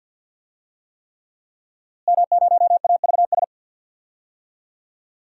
Morse code, heard as a keyed tone, about 52 wpm, 700 Hz. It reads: M0AVS
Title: DXCC-100 Call Signs - 5 Characters - Repeated 3x - 50wpm